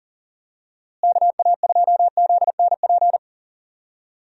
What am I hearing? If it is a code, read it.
Morse code, heard as a keyed tone, about 40 wpm, 700 Hz. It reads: KA2ZNP
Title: USA Call Signs - 6 Characters - Repeated 2x - 40wpm